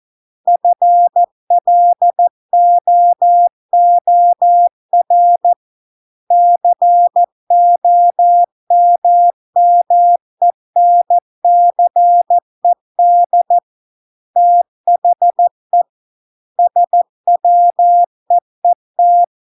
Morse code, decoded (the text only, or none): FLOOR COMMENCED THE SWEET